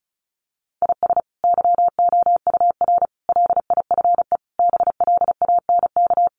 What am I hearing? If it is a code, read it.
Morse code, heard as a keyed tone, about 34 wpm, 700 Hz. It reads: IS YOUR LIFE BLANK
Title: Sentences from Top 1500 Words 35wpm